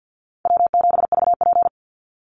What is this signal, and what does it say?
Morse code, heard as a keyed tone, about 49 wpm, 700 Hz. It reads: W7VP
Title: Call Signs Repeated 3x - 50wpm